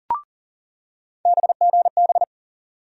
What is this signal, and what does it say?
Morse code, heard as a keyed tone, about 40 wpm, 700 Hz. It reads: BGB
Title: USA Call Sign Postfixes - 3 Letters 40wpm